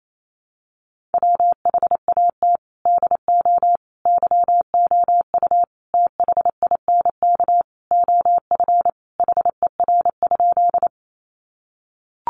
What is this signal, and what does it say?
Morse code, heard as a keyed tone, about 28 wpm, 700 Hz. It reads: WHAT DO YOU THINK OF HER?
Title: Sentences from Top 200 Words 28wpm